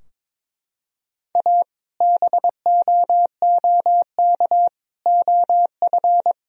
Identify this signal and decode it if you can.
Morse code, heard as a keyed tone, about 22 wpm, 700 Hz. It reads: A BOOK OF
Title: Sentences from Top 1000 Words - Review Part 2 - 22wpm